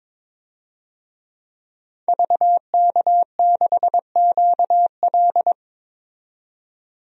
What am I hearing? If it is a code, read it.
Morse code, heard as a keyed tone, about 21 wpm, 700 Hz. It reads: VK6QL
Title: DXCC-100 Call Signs - 5 Characters - 22wpm